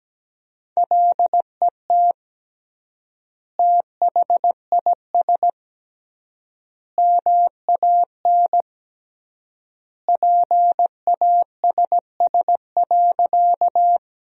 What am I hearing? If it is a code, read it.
Morse code, heard as a keyed tone, about 17 wpm, 700 Hz. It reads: LET THIS MAN PASS.
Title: 3x Word-Spacing - Sentences from Top 400 Words - 17wpm